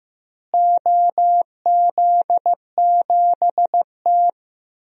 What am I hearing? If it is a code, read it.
Morse code, heard as a keyed tone, about 16 wpm, 700 Hz. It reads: OZ7T
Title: DXCC-100 Call Signs - 4 Characters - Repeated 2x - 15wpm